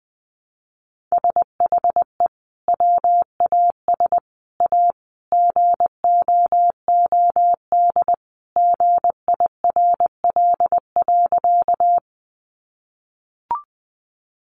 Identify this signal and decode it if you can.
Morse code, heard as a keyed tone, about 19 wpm, 700 Hz. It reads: SHE WAS A GOOD GIRL.
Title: Sentences from Top 400 Words 20wpm